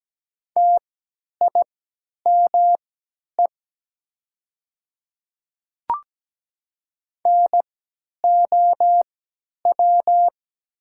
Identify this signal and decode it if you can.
Morse code, heard as a keyed tone, about 17 wpm, spaced as for 10 wpm, 700 Hz. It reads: TIME NOW
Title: 3x Character Spacing - Top 200 Words - 17wpm